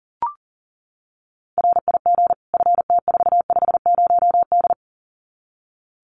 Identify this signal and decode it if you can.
Morse code, heard as a keyed tone, about 40 wpm, 700 Hz. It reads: RIG FT450D
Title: Common QSO Phrases 40wpm